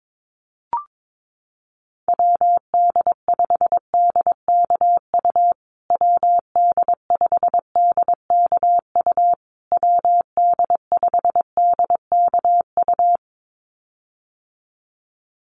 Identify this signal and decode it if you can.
Morse code, heard as a keyed tone, about 22 wpm, 700 Hz. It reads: WD5DKU WD5DKU WD5DKU
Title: USA Call Signs Repeated 3x 22wpm